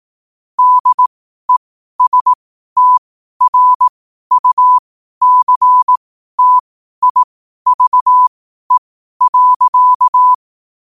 Morse code, heard as a keyed tone, about 18 wpm, spaced as for 12 wpm, 1000 Hz. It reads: DESTRUCTIVE.
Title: 12 WPM morse code quotes for Fri, 01 Mar 2024